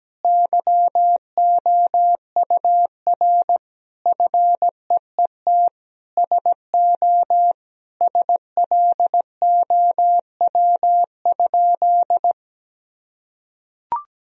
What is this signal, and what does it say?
Morse code, heard as a keyed tone, about 17 wpm, 700 Hz. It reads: YOUR FEET SO SLOW?
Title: Sentences from Top 1000 Words - Review Part 2 - 17wpm